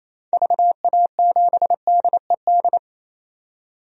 Morse code, heard as a keyed tone, about 28 wpm, 700 Hz. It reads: VA7DED
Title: DXCC-100 Call Signs - 6 Characters - 28wpm